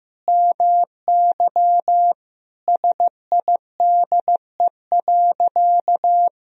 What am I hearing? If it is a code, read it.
Morse code, heard as a keyed tone, about 16 wpm, 700 Hz. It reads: MY SIDE.
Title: Sentences from Top 1000 Words - Review Part 3 - 15wpm